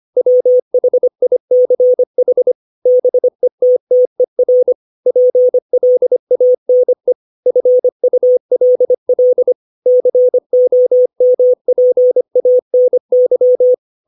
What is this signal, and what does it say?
Morse code, heard as a keyed tone, about 25 wpm, 500 Hz. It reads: WHICH BETTER PLANE FULL COMPANY
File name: 632 words / 632 Words 25WPM